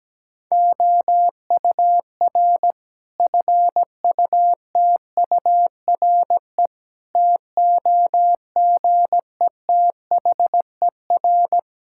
Morse code, heard as a keyed tone, about 17 wpm, 700 Hz. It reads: OUR FUTURE TOGETHER
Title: Sentences from Top 1000 Words - Review Part 5 - 17wpm